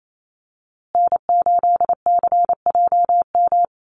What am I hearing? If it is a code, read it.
Morse code, heard as a keyed tone, about 28 wpm, 700 Hz. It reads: N8CJM
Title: USA Call Signs Repeated 3x 28wpm